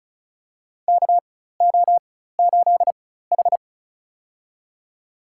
Morse code, heard as a keyed tone, about 36 wpm, spaced as for 16 wpm, 700 Hz. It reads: KO8H
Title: USA Call Signs - 4 Characters - 4x Character Spacing - 35wpm